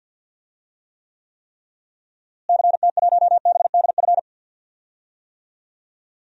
Morse code, heard as a keyed tone, about 47 wpm, 700 Hz. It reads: CT1BDF
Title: DXCC-100 Call Signs - 6 Characters - Repeated 3x - 50wpm